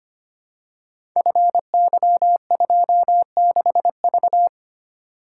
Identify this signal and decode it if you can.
Morse code, heard as a keyed tone, about 25 wpm, 700 Hz. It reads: FY26V
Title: Code Groups - 5 Letter:Numbers - 25wpm